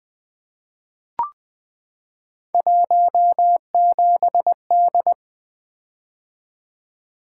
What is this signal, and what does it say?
Morse code, heard as a keyed tone, about 20 wpm, 700 Hz. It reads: 17D
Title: Three Letter-Number Combinations - 20wpm